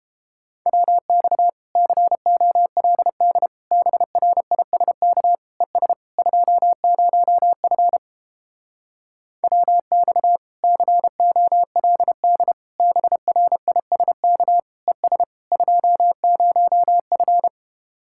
Morse code, heard as a keyed tone, about 31 wpm, 700 Hz. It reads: WX COLD BRISK ES 20F WX COLD BRISK ES 20F